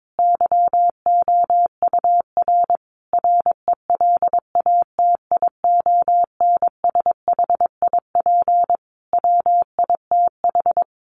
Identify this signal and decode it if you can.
Morse code, heard as a keyed tone, about 22 wpm, 700 Hz. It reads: YOUR RELATIONSHIP WITH
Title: Sentences from Top 1100 Words 22wpm